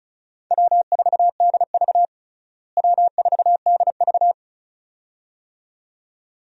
Morse code, heard as a keyed tone, about 36 wpm, 700 Hz. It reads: W4DV W4DV
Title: USA Call Signs - 4 Characters - Repeated 2x - 35wpm